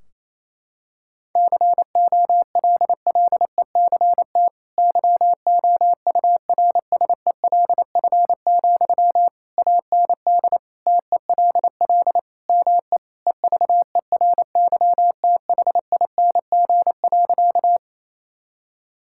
Morse code, heard as a keyed tone, about 28 wpm, 700 Hz. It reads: COLLECT YOURSELF, AND TELL ME EVERYTHING.
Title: Sentences from Top 1600 Words 28wpm